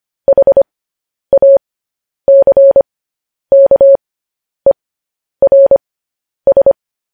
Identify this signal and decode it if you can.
Morse code, heard as a keyed tone, about 25 wpm, spaced as for 10 wpm, 550 Hz. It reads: HACKERS
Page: news-headlines-in-morse-code-at-10-wpm